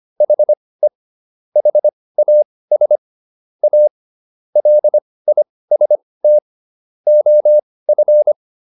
Morse code, heard as a keyed tone, about 25 wpm, spaced as for 18 wpm, 600 Hz. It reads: HE HAS A LIST OF